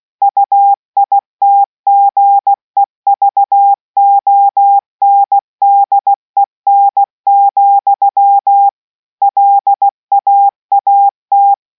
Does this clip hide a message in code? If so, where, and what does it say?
Yes: Morse code from 0.2 to 11.5 s, heard as a keyed tone, about 16 wpm, 800 Hz. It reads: UITGEVONDEN, LAAT